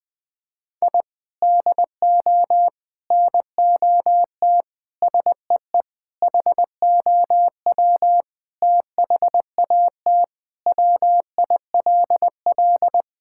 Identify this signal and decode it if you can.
Morse code, heard as a keyed tone, about 21 wpm, 700 Hz. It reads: I DO NOT SEE HOW THAT WILL
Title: Sentences with 5-Word Phrases - Spoken Beginning - 20wpm